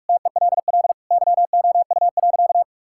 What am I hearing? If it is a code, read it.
Morse code, heard as a keyed tone, about 46 wpm, 700 Hz. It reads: TELL YOU.
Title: Sentences from Top 1000 Words - Review Part 5 - 45wpm